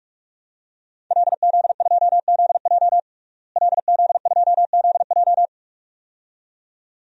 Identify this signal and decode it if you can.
Morse code, heard as a keyed tone, about 46 wpm, 700 Hz. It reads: LZ2ZJ LZ2ZJ
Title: DXCC-100 Call Signs - 5 Characters - Repeated 2x - 45wpm